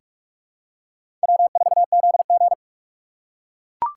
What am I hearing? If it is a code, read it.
Morse code, heard as a keyed tone, about 41 wpm, 700 Hz. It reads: W4ZG
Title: USA Call Signs - 4 Characters - 3x Character Spacing - 45wpm